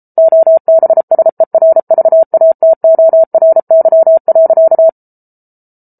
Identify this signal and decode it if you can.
Morse code, heard as a keyed tone, about 34 wpm, 650 Hz. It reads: OBSERVATORY.